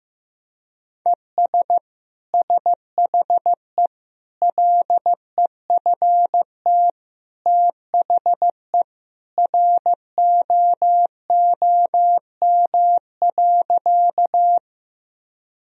Morse code, heard as a keyed tone, about 15 wpm, 700 Hz. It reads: ES SHE LEFT THE ROOM.
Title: Sentences from Top 300 Words and Abbreviations 15wpm